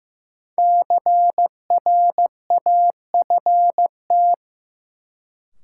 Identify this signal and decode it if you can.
Morse code, heard as a keyed tone, about 15 wpm, 700 Hz. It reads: CRAFT
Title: Sentences from Top 1000 Words - Review Part 6 - New Word Drill - 15wpm